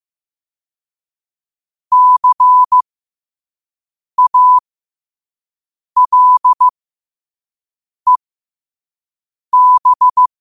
Morse code, heard as a keyed tone, about 15 wpm, spaced as for 5 wpm, 1000 Hz. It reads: CALEB
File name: qotdfiles / 5wpm